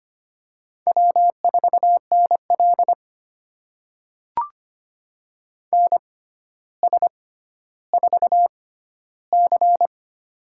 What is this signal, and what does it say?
Morse code, heard as a keyed tone, about 25 wpm, spaced as for 13 wpm, 700 Hz. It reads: W4NL NS4C
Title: USA Call Signs - 4 Characters - 6x Character Spacing - 25wpm